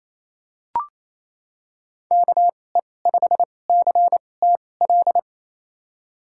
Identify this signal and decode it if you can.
Morse code, heard as a keyed tone, about 28 wpm, spaced as for 20 wpm, 700 Hz. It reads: KE5CTL
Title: USA Call Signs - 6 Characters - 2x Character Spacing - 28wpm